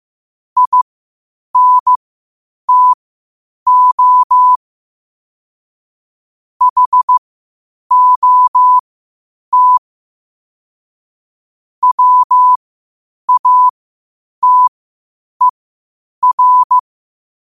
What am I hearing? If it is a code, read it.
Morse code, heard as a keyed tone, about 15 wpm, spaced as for 8 wpm, 1000 Hz. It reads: INTO HOT WATER
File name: qotdfiles / 8wpm